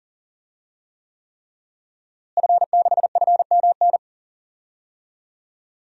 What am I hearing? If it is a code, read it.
Morse code, heard as a keyed tone, about 37 wpm, 700 Hz. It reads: F6FMN
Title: DXCC-100 Call Signs - 5 Characters - Repeated 3x - 40wpm